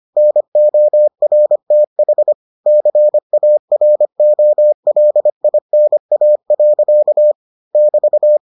Morse code, heard as a keyed tone, about 25 wpm, 600 Hz. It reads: NORTH CAROLINA. <BT>